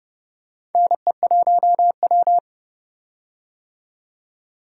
Morse code, heard as a keyed tone, about 30 wpm, 700 Hz. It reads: NE1W
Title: USA Call Signs - 4 Characters - 30wpm